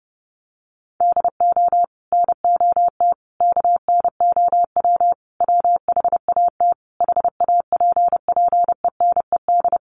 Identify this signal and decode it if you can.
Morse code, heard as a keyed tone, about 30 wpm, 700 Hz. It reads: DO NOT KNOW WHAT HAPPENED
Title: Sets of 5 Words Encore2 30wpm